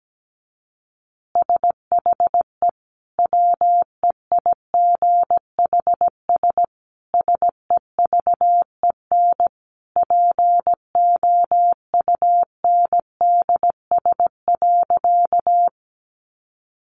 Morse code, heard as a keyed tone, about 17 wpm, 700 Hz. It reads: SHE WEIGHS SEVEN POUNDS.
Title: Easy Phrases 17wpm